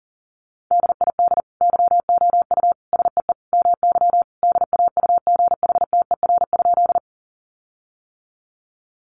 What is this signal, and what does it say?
Morse code, heard as a keyed tone, about 40 wpm, 700 Hz. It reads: DID YOU SEE MY DAUGHTER?
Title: Sentences from Top 1500 Words 40wpm